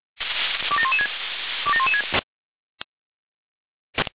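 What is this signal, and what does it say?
Five-tone selective call (ZVEI), heard as quick runs of five tones, tones of about 70 ms. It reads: 27116 27116
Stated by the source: ZVEI five-tone selective call received by SDR.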